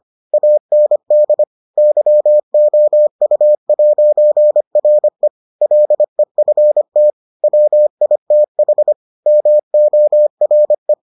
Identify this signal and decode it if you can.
Morse code, heard as a keyed tone, about 25 wpm, 600 Hz. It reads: AND YOU'RE LEFT WITH MORE